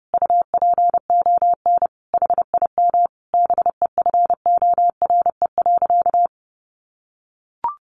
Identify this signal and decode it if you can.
Morse code, heard as a keyed tone, about 30 wpm, 700 Hz. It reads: UPON HIM BEFORE.
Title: Sentences from Top 1100 Words 30wpm